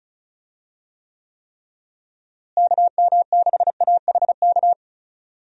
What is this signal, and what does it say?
Morse code, heard as a keyed tone, about 34 wpm, 700 Hz. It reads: KM6AHK
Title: USA Call Signs - 6 Characters - Repeated 3x - 35wpm